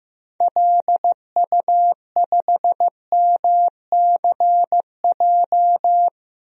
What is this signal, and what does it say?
Morse code, heard as a keyed tone, about 15 wpm, 700 Hz. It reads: LU5MCJ
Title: DXCC-100 Call Signs - 6 Characters - 15wpm